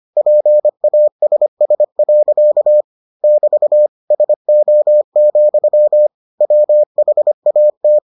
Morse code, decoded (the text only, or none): PASS. <BT> SO, WHAT